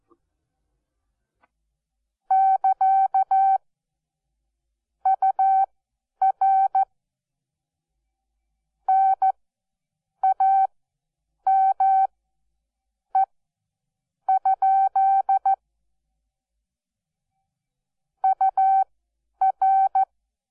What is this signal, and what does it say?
Morse code, heard as a keyed tone, about 14 wpm, spaced as for 8 wpm, 770 Hz. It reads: <KA> UR NAME? UR
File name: Onlinelehrgang / morsen / DJ4UF Morsen12